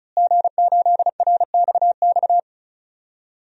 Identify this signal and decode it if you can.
Morse code, heard as a keyed tone, about 34 wpm, 700 Hz. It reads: G8RXX
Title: DXCC-100 Call Signs - 5 Characters - Repeated 2x - 35wpm